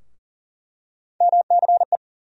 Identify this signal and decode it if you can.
Morse code, heard as a keyed tone, about 40 wpm, 700 Hz. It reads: MCE
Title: USA Call Sign Postfixes - 3 Letters 40wpm